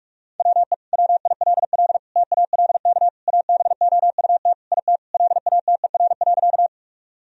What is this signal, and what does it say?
Morse code, heard as a keyed tone, about 41 wpm, 700 Hz. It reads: WE WILL TALK ABOUT IT LATER.
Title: Sentences from Top 1000 Words - Review Part 4 - 45wpm